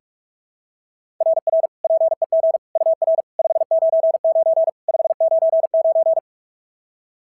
Morse code, heard as a keyed tone, about 46 wpm, 690 Hz. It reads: RR PEG UR 599 599
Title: SOTA QSO Practice 45wpm